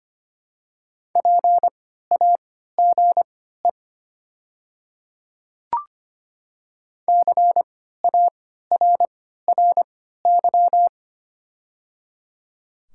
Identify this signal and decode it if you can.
Morse code, heard as a keyed tone, about 25 wpm, spaced as for 14 wpm, 700 Hz. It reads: PAGE CARRY
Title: 3x Character Spacing - Top 300 Words - 25wpm